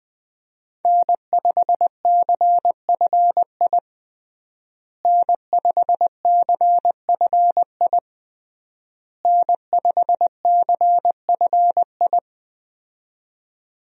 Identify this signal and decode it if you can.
Morse code, heard as a keyed tone, about 20 wpm, 700 Hz. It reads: N5CFI N5CFI N5CFI
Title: USA Call Signs - 5 Characters - Repeated 3x - 20wpm